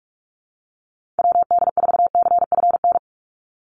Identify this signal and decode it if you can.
Morse code, heard as a keyed tone, about 44 wpm, 700 Hz. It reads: WD4CFN
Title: NA SOTA Chaser Callsigns 45wpm